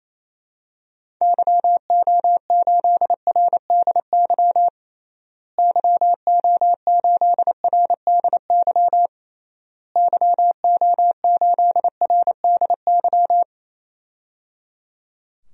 Morse code, heard as a keyed tone, about 28 wpm, 700 Hz. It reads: YO8RDY YO8RDY YO8RDY
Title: DXCC-100 Call Signs - 6 Characters - Repeated 3x - 28wpm